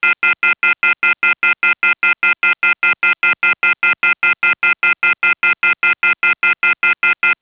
Off-hook warning tone, 1400 + 2060 + 2450 + 2600 Hz, with noise.
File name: things / PacBell 5E perm sig